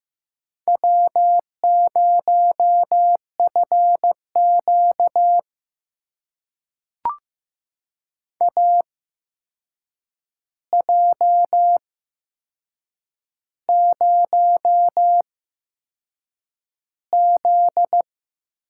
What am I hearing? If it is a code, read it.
Morse code, heard as a keyed tone, about 15 wpm, spaced as for 6 wpm, 700 Hz. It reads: W0FQ AJ0Z
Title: USA Call Signs - 4 Characters - 8x Character Spacing - 15wpm